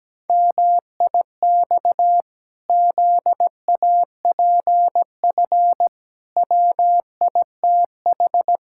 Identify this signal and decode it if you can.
Morse code, heard as a keyed tone, about 17 wpm, 700 Hz. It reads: MIX ZAPF WITH
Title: Pangram Challenge v2 - 17wpm